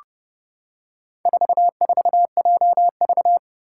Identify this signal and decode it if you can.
Morse code, heard as a keyed tone, about 30 wpm, 700 Hz. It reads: 44JV
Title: Four Challenging Letter-Number Combinations 30wpm